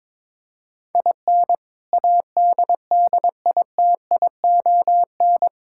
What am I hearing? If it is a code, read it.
Morse code, heard as a keyed tone, about 22 wpm, 700 Hz. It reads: IN ADDITION
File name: practice / 22wpm / Sentences with 3-Word Phrases - 22wpm